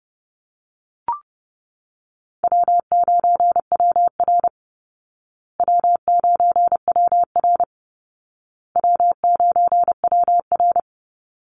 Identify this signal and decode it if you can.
Morse code, heard as a keyed tone, about 30 wpm, 700 Hz. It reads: W9WR W9WR W9WR
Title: Call Signs Repeated 2x - 30wpm